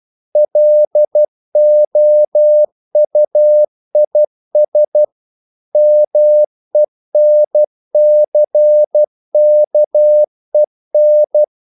Morse code, heard as a keyed tone, about 12 wpm, 600 Hz. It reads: LOUIS MENCKEN
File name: CW / Practice / EnglishQuotations0312